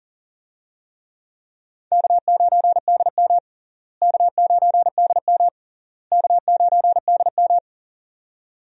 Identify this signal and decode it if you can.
Morse code, heard as a keyed tone, about 40 wpm, 700 Hz. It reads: K9DM K9DM K9DM
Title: USA Call Signs - 4 Characters - Repeated 3x - 40wpm